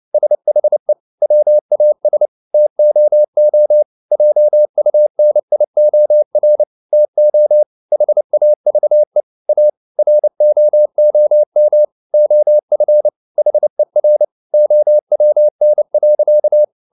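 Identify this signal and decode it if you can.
Morse code, heard as a keyed tone, about 29 wpm, 600 Hz. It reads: SHE WAS TOO JUNIOR TO HAVE A ROOM OF HER OWN.
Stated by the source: Morse code at 29 wpm, sending Ken Follett's novel Never in English, from CW med Gnister.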